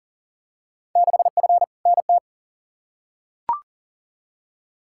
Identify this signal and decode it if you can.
Morse code, heard as a keyed tone, about 40 wpm, 700 Hz. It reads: 6F NT
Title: Field Day Exchanges Encore - 40wpm